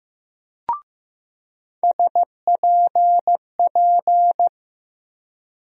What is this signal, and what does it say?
Morse code, heard as a keyed tone, about 15 wpm, 700 Hz. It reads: SPP
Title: USA Call Sign Postfixes - 3 Letters 15wpm